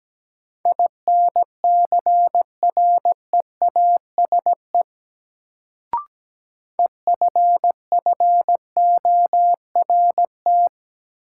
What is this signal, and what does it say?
Morse code, heard as a keyed tone, about 17 wpm, 700 Hz. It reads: INCREASE EFFORT
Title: Sentences from Top 1200 Words - New Word Drill Alt - 17wpm